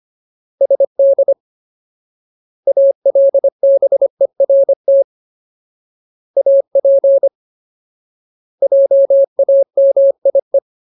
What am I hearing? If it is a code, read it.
Morse code, heard as a keyed tone, about 25 wpm, 550 Hz. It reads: SD ALBERT AP JAMIE
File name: Ultimate Word List / 25WPM Ultimate